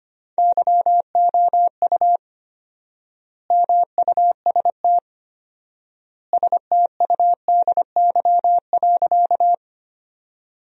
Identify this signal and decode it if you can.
Morse code, heard as a keyed tone, about 25 wpm, 700 Hz. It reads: YOU MUST STUDY.
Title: 4x Word-Spacing - Sentences from Top 300 Words - 25wpm